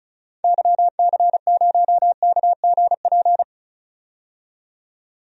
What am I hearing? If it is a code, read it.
Morse code, heard as a keyed tone, about 34 wpm, 700 Hz. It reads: YC0KGP
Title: DXCC-100 Call Signs - 6 Characters - 35wpm